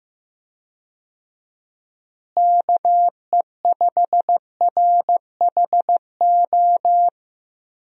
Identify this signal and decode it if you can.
Morse code, heard as a keyed tone, about 15 wpm, 700 Hz. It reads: KE5RHO
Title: USA Call Signs - 6 Characters - Repeated 3x - 15wpm